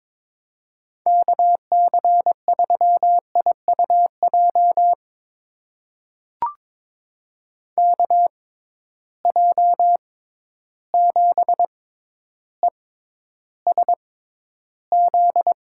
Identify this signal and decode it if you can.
Morse code, heard as a keyed tone, about 22 wpm, spaced as for 11 wpm, 700 Hz. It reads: KC3IUJ KJ7ESZ
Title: USA Call Signs - 6 Characters - 6x Character Spacing - 22wpm